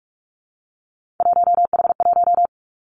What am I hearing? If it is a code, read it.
Morse code, heard as a keyed tone, about 46 wpm, 700 Hz. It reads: 1H1
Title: Three Challenging Letter-Number Combinations 45wpm